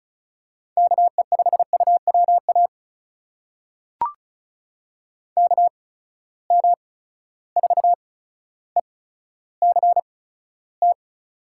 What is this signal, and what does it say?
Morse code, heard as a keyed tone, about 36 wpm, spaced as for 15 wpm, 700 Hz. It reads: KE5UWA KM4ECT
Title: USA Call Signs - 6 Characters - 8x Character Spacing - 35wpm